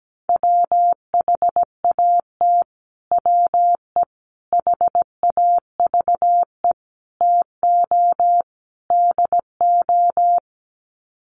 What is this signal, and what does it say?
Morse code, heard as a keyed tone, about 17 wpm, 700 Hz. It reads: WHAT WE HAVE TO DO
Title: Sets of 5 Words Encore 17wpm